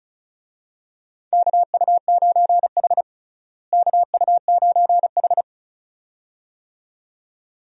Morse code, heard as a keyed tone, about 36 wpm, 700 Hz. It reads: KU9H KU9H
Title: USA Call Signs - 4 Characters - Repeated 2x - 35wpm